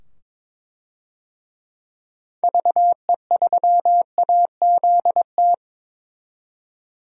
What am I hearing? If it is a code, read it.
Morse code, heard as a keyed tone, about 22 wpm, 700 Hz. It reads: VE3AZT
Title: DXCC-100 Call Signs - 6 Characters - 22wpm